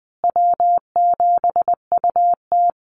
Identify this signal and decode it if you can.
Morse code, heard as a keyed tone, about 20 wpm, 700 Hz. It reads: W7UT
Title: Call Signs Repeated 3x - 20wpm